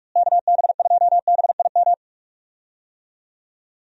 Morse code, heard as a keyed tone, about 46 wpm, 700 Hz. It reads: KB2BIM
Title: USA Call Signs - 6 Characters - Repeated 2x - 45wpm